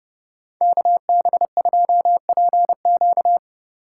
Morse code, heard as a keyed tone, about 29 wpm, 700 Hz. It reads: KB2PQ